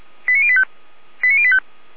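Five-tone selective call (ZVEI), heard as quick runs of five tones, tones of about 70 ms. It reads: 89085 89085
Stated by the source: ZVEI five-tone call received by SDR at feuerwehr-orlamuende.org (fire brigade Orlamünde).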